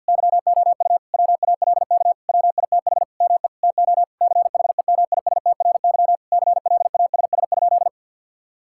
Morse code, heard as a keyed tone, about 50 wpm, 700 Hz. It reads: YOU WALK WITH ME TO CHEMISTRY CLASS?
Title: Sentences from Top 1000 Words - Review Part 6 - 50wpm